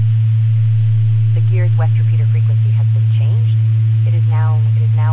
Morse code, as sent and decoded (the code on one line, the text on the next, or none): none